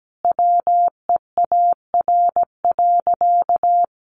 Morse code, heard as a keyed tone, about 17 wpm, 700 Hz. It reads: WEAR.
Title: Sentences from Top 1500 Words 17wpm